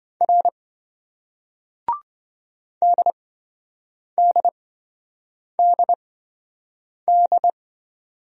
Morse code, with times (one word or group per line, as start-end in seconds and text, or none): 0.2-0.5 R
2.8-7.5 DDDD